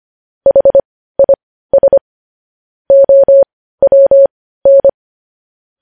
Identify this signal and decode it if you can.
Morse code, heard as a keyed tone, about 25 wpm, spaced as for 15 wpm, 550 Hz. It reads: HIS OWN
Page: news-headlines-in-morse-code-at-15-wpm